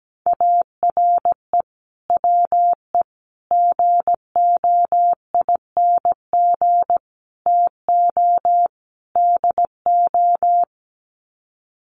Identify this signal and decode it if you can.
Morse code, heard as a keyed tone, about 17 wpm, 700 Hz. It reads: ARE WE GOING TO DO
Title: Sets of 5 Words 17wpm